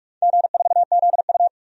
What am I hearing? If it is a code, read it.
Morse code, heard as a keyed tone, about 41 wpm, 700 Hz. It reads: G4ZU
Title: DXCC-100 Call Signs - 4 Characters - 45wpm